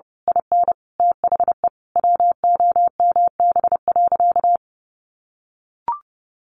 Morse code, heard as a keyed tone, about 28 wpm, 700 Hz. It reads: IN THE WOMB.
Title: A Sound Of Thunder 30wpm